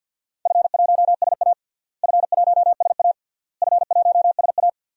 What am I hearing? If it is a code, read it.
Morse code, heard as a keyed tone, about 51 wpm, 700 Hz. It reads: F1SA F1SA F1SA
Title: DXCC-100 Call Signs - 4 Characters - Repeated 3x - 50wpm